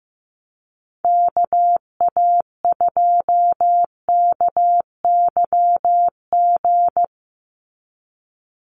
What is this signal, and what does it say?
Morse code, heard as a keyed tone, about 15 wpm, 700 Hz. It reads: KA2KYG
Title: USA Call Signs 15wpm